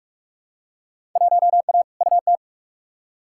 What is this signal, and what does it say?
Morse code, heard as a keyed tone, about 47 wpm, 700 Hz. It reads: 1A UT
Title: Field Day Exchanges - 45wpm